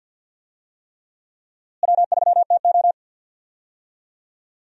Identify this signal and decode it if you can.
Morse code, heard as a keyed tone, about 48 wpm, 700 Hz. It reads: W3TO
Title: USA Call Signs - 4 Characters - Repeated 3x - 50wpm